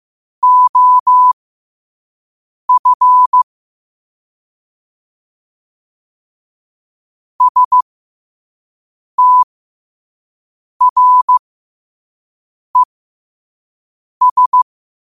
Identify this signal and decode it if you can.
Morse code, heard as a keyed tone, about 15 wpm, spaced as for 5 wpm, 1000 Hz. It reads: OF STRES
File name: qotdfiles / 5wpm